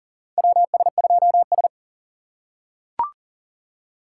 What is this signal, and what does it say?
Morse code, heard as a keyed tone, about 40 wpm, 700 Hz. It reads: WS2S
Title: USA Call Signs - 4x Character Spacing - 40wpm